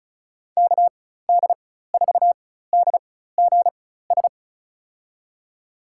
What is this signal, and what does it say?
Morse code, heard as a keyed tone, about 36 wpm, spaced as for 16 wpm, 700 Hz. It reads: KD4DGS
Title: USA Call Signs - 6 Characters - 4x Character Spacing - 35wpm